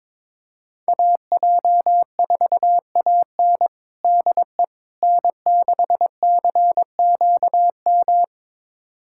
Morse code, heard as a keyed tone, about 22 wpm, 700 Hz. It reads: AJ4AN DE N6CQM
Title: USA Callsign DE Callsign - 5 Characters - 22wpm